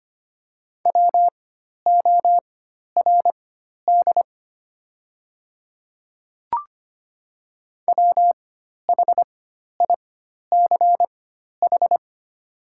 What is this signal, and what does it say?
Morse code, heard as a keyed tone, about 25 wpm, spaced as for 12 wpm, 690 Hz. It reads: WORD WHICH
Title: 4x Character Spacing - Top 100 Words - 25wpm